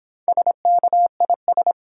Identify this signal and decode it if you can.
Morse code, heard as a keyed tone, about 26 wpm, 700 Hz. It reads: SKIS